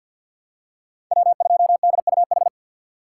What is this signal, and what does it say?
Morse code, heard as a keyed tone, about 48 wpm, 700 Hz. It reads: W 2 D U H
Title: USA Call Signs - 5 Characters - 6x Character Spacing - 50wpm